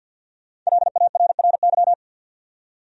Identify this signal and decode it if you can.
Morse code, heard as a keyed tone, about 52 wpm, 700 Hz. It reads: LARRY
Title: USA Common Names - Max 5 Letters - 50wpm